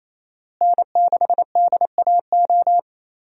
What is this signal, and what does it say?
Morse code, heard as a keyed tone, about 28 wpm, 700 Hz. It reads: N6DAO
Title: USA Call Signs - 5 Characters - Repeated 2x - 28wpm